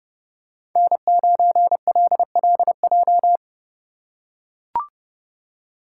Morse code, heard as a keyed tone, about 28 wpm, 700 Hz. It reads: N9LLJ